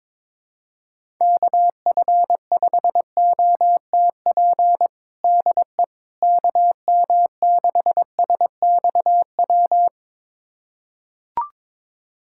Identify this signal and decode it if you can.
Morse code, heard as a keyed tone, about 22 wpm, 700 Hz. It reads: KF5OTP DE KM6SXW
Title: USA Callsign DE Callsign 22wpm